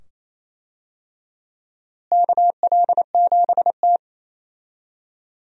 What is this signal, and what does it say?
Morse code, heard as a keyed tone, about 28 wpm, 700 Hz. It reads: KL7T
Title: DXCC-100 Call Signs - 4 Characters - Repeated 2x - 28wpm